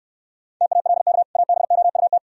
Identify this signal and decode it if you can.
Morse code, heard as a keyed tone, about 68 wpm, 700 Hz. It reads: TALK ABOUT